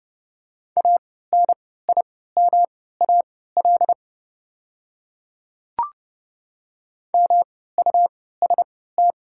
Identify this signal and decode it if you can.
Morse code, heard as a keyed tone, about 29 wpm, spaced as for 17 wpm, 690 Hz. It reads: ANIMAL MUST
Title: 3x Character Spacing - Top 200 Words - 30wpm